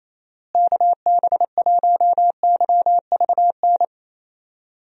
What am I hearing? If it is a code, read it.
Morse code, heard as a keyed tone, about 27 wpm, 700 Hz. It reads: KB1YVN
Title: USA Call Signs - 6 Characters - Repeated 2x - 28wpm